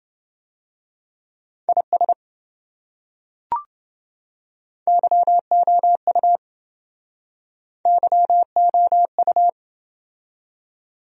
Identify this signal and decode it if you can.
Morse code, heard as a keyed tone, about 25 wpm, 700 Hz. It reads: IS YOU YOU
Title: Speed-Racing Top 100 Words 20wpm